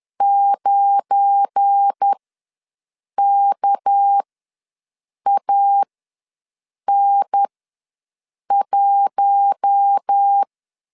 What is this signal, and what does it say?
Morse code, heard as a keyed tone, about 10 wpm, spaced as for 6 wpm, 800 Hz. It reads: 9KAN1
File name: LEZIONI / 7 GF9Z / 7 GF9 ws 10wpm 4034